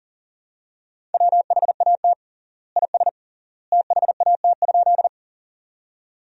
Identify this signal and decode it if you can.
Morse code, heard as a keyed tone, about 37 wpm, 700 Hz. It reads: WHAT IS THAT?
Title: 3x Word-Spacing - Sentences from Top 100 Words - 40wpm